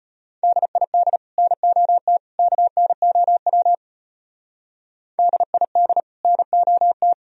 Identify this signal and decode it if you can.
Morse code, heard as a keyed tone, about 32 wpm, 700 Hz. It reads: DID NOT KNOW DID NOT